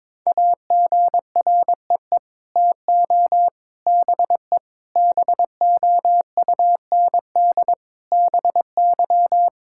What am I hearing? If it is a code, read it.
Morse code, heard as a keyed tone, about 22 wpm, 700 Hz. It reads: AGREE TO BE BOUND BY
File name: Sets of 9 Words - 22wpm